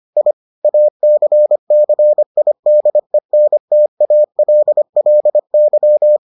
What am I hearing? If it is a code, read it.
Morse code, heard as a keyed tone, about 25 wpm, 600 Hz. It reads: I ACCIDENTALLY